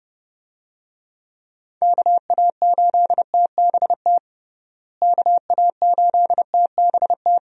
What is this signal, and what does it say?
Morse code, heard as a keyed tone, about 30 wpm, 700 Hz. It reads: KA8TBT KA8TBT
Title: USA Call Signs - 6 Characters - Repeated 2x - 30wpm